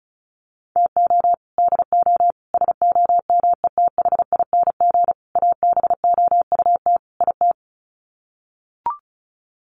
Morse code, heard as a keyed tone, about 34 wpm, 700 Hz. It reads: TO DO SOMETHING ABOUT IT
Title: Sets of 5 Words 35wpm